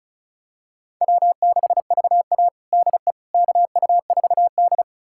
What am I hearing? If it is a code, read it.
Morse code, heard as a keyed tone, about 34 wpm, 700 Hz. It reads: W6VA DE KU4D
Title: USA Callsign DE Callsign - 4 Characters - 35wpm